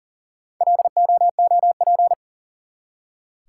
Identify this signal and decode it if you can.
Morse code, heard as a keyed tone, about 39 wpm, 700 Hz. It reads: LOOP
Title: Top 180 QSO Elements 40wpm